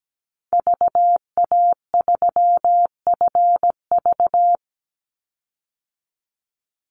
Morse code, heard as a keyed tone, about 17 wpm, 700 Hz. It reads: VA3FV
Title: Call Signs - 17wpm